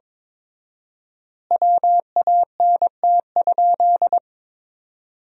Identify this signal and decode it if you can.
Morse code, heard as a keyed tone, about 22 wpm, 700 Hz. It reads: WANT?
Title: 4x Word-Spacing - Sentences from Top 200 Words - 22wpm